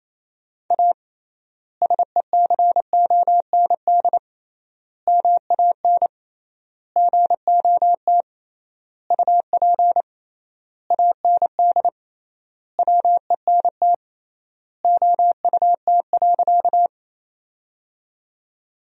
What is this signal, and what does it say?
Morse code, heard as a keyed tone, about 28 wpm, 700 Hz. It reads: A SECOND MAN GOT UP AND WENT OUT.
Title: 3x Word-Spacing - Sentences from Top 300 Words - 28wpm